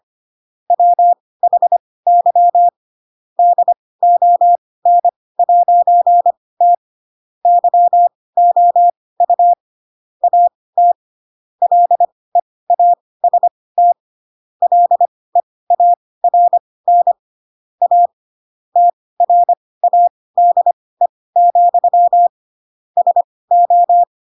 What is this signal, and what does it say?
Morse code, heard as a keyed tone, about 25 wpm, spaced as for 18 wpm, 700 Hz. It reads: WHY DON'T YOU AT LEAST LEARN A TRADE, SO
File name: Pinocchio / Pinocchio ch4 12 25 ik1hge